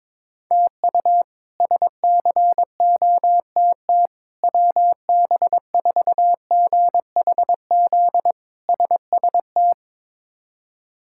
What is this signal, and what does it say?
Morse code, heard as a keyed tone, about 22 wpm, 700 Hz. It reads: TU SCOTT WB4GHZ SST
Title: K1USN SST QSO Practice - 22wpm